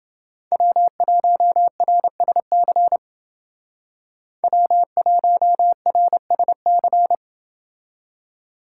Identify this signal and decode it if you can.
Morse code, heard as a keyed tone, about 28 wpm, 700 Hz. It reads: W1RSC W1RSC